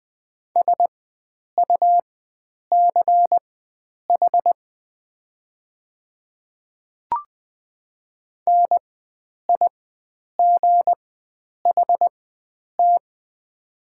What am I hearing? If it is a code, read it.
Morse code, heard as a keyed tone, about 20 wpm, spaced as for 9 wpm, 700 Hz. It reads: SUCH NIGHT
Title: 4x Character Spacing - Top 300 Words - 20wpm